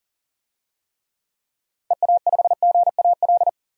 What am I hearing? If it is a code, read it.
Morse code, heard as a keyed tone, about 40 wpm, 700 Hz. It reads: EA5GAL
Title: DXCC-100 Call Signs - 6 Characters - 40wpm